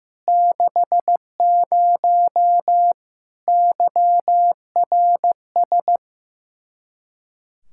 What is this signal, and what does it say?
Morse code, heard as a keyed tone, about 16 wpm, 700 Hz. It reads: 60 YRS
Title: K1USN SST QSO Practice 15wpm